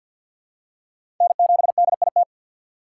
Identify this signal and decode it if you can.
Morse code, heard as a keyed tone, about 52 wpm, 700 Hz. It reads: N7DIT
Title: USA Call Signs - 5 Characters - 2x Character Spacing - 50wpm